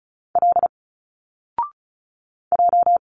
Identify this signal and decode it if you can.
Morse code, heard as a keyed tone, about 34 wpm, 700 Hz. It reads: L J
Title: Single Letters 35wpm